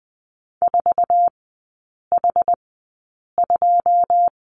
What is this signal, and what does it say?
Morse code, heard as a keyed tone, about 20 wpm, spaced as for 8 wpm, 690 Hz. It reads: 4H2
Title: 093 Lesson 16e - Letter c - Three Characters 20wpm